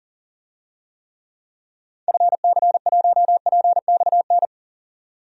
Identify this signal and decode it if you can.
Morse code, heard as a keyed tone, about 40 wpm, 700 Hz. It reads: FC1PXN
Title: DXCC-100 Call Signs - 6 Characters - 40wpm